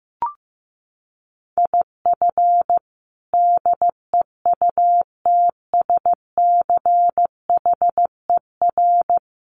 Morse code, heard as a keyed tone, about 15 wpm, 700 Hz. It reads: IF DEUTSCHER
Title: A Sound Of Thunder 15wpm